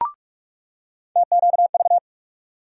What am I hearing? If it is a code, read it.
Morse code, heard as a keyed tone, about 42 wpm, 700 Hz. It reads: TQV
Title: USA Call Sign Postfixes - 3 Letters 45wpm